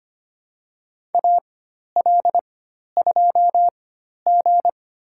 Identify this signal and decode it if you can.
Morse code, heard as a keyed tone, about 25 wpm, spaced as for 12 wpm, 700 Hz. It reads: AL2G
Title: USA Call Signs - 4 Characters - 4x Character Spacing - 25wpm